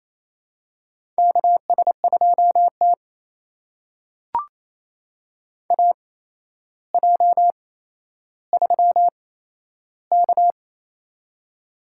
Morse code, heard as a keyed tone, about 27 wpm, spaced as for 12 wpm, 700 Hz. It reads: KS2T AJ3K
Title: USA Call Signs - 4 Characters - 8x Character Spacing - 28wpm